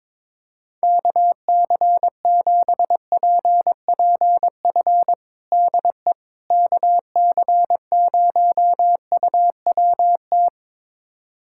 Morse code, heard as a keyed tone, about 22 wpm, 700 Hz. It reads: KC7PPF DE KC0UWT
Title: USA Callsign DE Callsign - 6 Characters - 22wpm